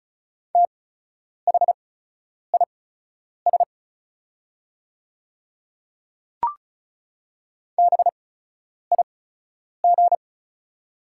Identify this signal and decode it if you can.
Morse code, heard as a keyed tone, about 34 wpm, spaced as for 10 wpm, 690 Hz. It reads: THIS BIG
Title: 8x Character Spacing - Top 200 Words - 35wpm